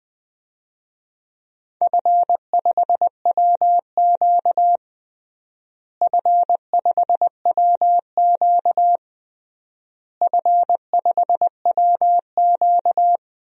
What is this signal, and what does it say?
Morse code, heard as a keyed tone, about 20 wpm, 700 Hz. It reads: F5WQ F5WQ F5WQ
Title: DXCC-100 Call Signs - 4 Characters - Repeated 3x - 20wpm